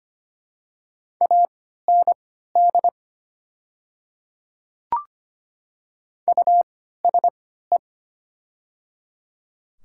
Morse code, heard as a keyed tone, about 25 wpm, spaced as for 14 wpm, 700 Hz. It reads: AND USE
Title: 3x Character Spacing - Top 100 Words - 25wpm